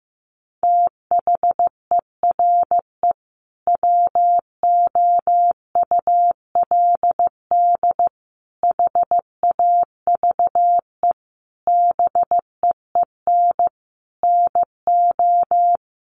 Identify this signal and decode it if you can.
Morse code, heard as a keyed tone, about 15 wpm, 700 Hz. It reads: THERE WOULD HAVE BEEN NO
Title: Sentences from Top 1400 Words 15wpm